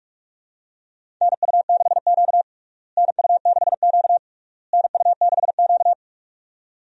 Morse code, heard as a keyed tone, about 42 wpm, 700 Hz. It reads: NU6Q NU6Q NU6Q
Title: USA Call Signs - 4 Characters - Repeated 3x - 45wpm